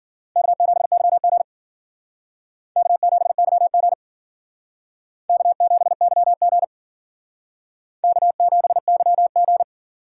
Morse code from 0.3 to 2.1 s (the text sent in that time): K7YG